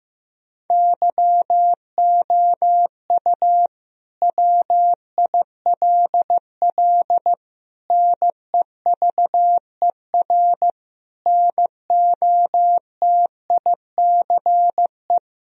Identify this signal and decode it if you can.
Morse code, heard as a keyed tone, about 15 wpm, 700 Hz. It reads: YOU WILL NEVER NOTICE
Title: Sentences from Top 1000 Words - Review Part 2 - 15wpm